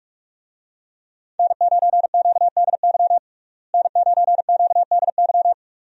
Morse code, heard as a keyed tone, about 46 wpm, 700 Hz. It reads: N9QDY N9QDY
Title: USA Call Signs - 5 Characters - Repeated 3x - 45wpm